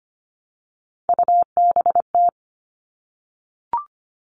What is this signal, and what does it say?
Morse code, heard as a keyed tone, about 25 wpm, 700 Hz. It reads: UBT
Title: All Cut Numbers - 3-Digit 25wpm